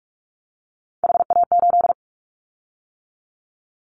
Morse code, heard as a keyed tone, about 46 wpm, 700 Hz. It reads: HA8
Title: Callsign Prefixes for Top 25 DXCC - 45wpm